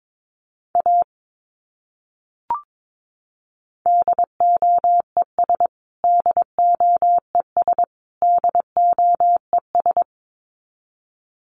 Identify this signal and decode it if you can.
Morse code, heard as a keyed tone, about 22 wpm, 700 Hz. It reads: A DOES DOES DOES
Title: Top 200 Words Repeated 22wpm